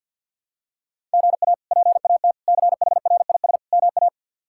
Morse code, heard as a keyed tone, about 51 wpm, 700 Hz. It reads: GA PAT CHRIS MA
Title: K1USN SST QSO Practice - 50wpm